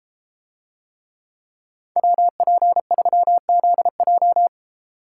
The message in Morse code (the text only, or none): WP3ZJ